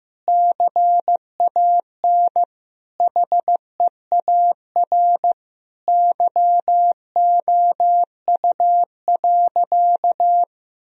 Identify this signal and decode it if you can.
Morse code, heard as a keyed tone, about 15 wpm, 700 Hz. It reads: CAN HEAR YOU.
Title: Sentences from Top 1000 Words - Review Part 1 - 15wpm